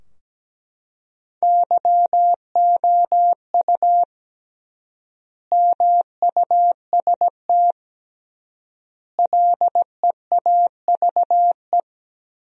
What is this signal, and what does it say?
Morse code, heard as a keyed tone, about 17 wpm, 700 Hz. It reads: YOU MUST LEAVE
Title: 3x Word-Spacing - Sentences from Top 400 Words - 17wpm